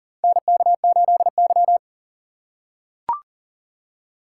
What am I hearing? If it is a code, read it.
Morse code, heard as a keyed tone, about 40 wpm, 700 Hz. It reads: NK8Y
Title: USA Call Signs - 4 Characters - 4x Character Spacing - 40wpm